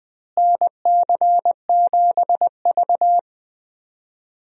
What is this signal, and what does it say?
Morse code, heard as a keyed tone, about 20 wpm, 700 Hz. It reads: NC7V
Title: USA Call Signs - 4 Characters - Repeated 2x - 20wpm